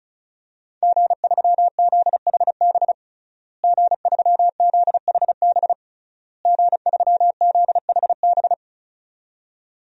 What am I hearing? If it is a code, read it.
Morse code, heard as a keyed tone, about 34 wpm, 700 Hz. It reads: G3ZHB G3ZHB G3ZHB
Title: DXCC-100 Call Signs - 5 Characters - Repeated 3x - 35wpm